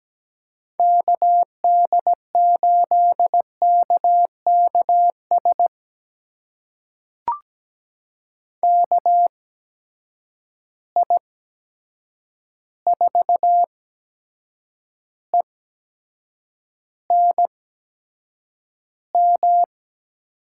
Morse code, heard as a keyed tone, about 17 wpm, spaced as for 7 wpm, 700 Hz. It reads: KD8KKS KI4ENM
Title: USA Call Signs - 6 Characters - 8x Character Spacing - 17wpm